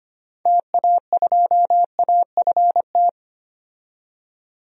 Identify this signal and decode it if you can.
Morse code, heard as a keyed tone, about 25 wpm, 700 Hz. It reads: TA2AFT
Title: DXCC-100 Call Signs - 6 Characters - 25wpm